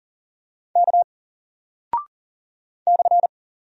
Morse code, heard as a keyed tone, about 40 wpm, 700 Hz. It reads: K /
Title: Single Character (Letter:Number:Punctuation:Prosign) 40wpm